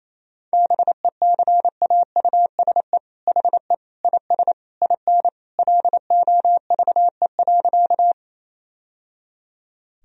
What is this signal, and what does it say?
Morse code, heard as a keyed tone, about 28 wpm, 700 Hz. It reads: BECAUSE HE IS IN LOVE.
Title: Sentences from Top 1000 Words - Review Part 3 - 28wpm